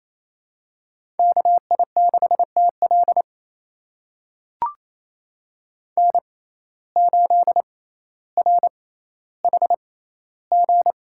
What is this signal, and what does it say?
Morse code, heard as a keyed tone, about 28 wpm, spaced as for 14 wpm, 700 Hz. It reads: KI6TL N8RHG
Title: USA Call Signs - 5 Characters - 6x Character Spacing - 28wpm